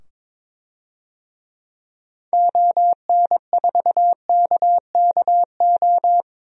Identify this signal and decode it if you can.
Morse code, heard as a keyed tone, about 22 wpm, 700 Hz. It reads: ON4KKO
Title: DXCC-100 Call Signs - 6 Characters - 22wpm